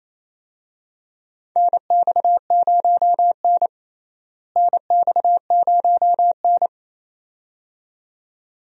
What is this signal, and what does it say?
Morse code, heard as a keyed tone, about 28 wpm, 700 Hz. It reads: NX0N NX0N
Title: USA Call Signs - 4 Characters - Repeated 2x - 28wpm